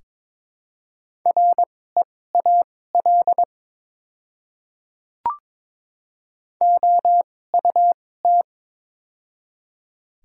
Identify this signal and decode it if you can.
Morse code, heard as a keyed tone, about 22 wpm, spaced as for 16 wpm, 690 Hz. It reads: REAL OUT
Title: 2x Character Spacing - Top 300 Words - 22wpm